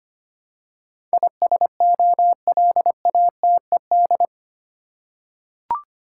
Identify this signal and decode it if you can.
Morse code, heard as a keyed tone, about 25 wpm, 700 Hz. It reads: ISOLATED
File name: 8 Letter Words - First 3 Letters Given - 25wpm